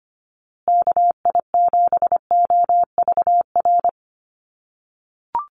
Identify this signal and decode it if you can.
Morse code, heard as a keyed tone, about 25 wpm, 700 Hz. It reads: KI7OVR
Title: USA Call Signs Repeated 3x 25wpm